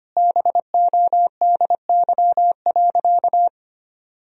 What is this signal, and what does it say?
Morse code, heard as a keyed tone, about 25 wpm, 700 Hz. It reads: BODY.
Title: Sentences from Top 1000 Words - Review Part 2 - 25wpm